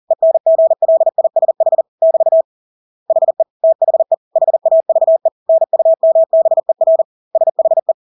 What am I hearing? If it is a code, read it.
Morse code, heard as a keyed tone, about 40 wpm, 650 Hz. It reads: ENGLISH <BT> HE THE HAVE NUMBER SHE
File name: Engelske ord 40-40 wpm